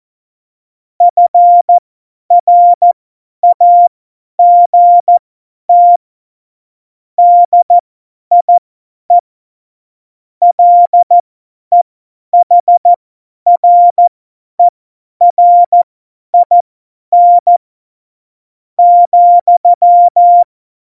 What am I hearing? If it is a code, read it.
Morse code, heard as a keyed tone, about 14 wpm, spaced as for 9 wpm, 700 Hz. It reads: FRAGT DIE LEHRERIN,
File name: Morsekurs / Witze / Witz 21. Januar 2020, 50 F70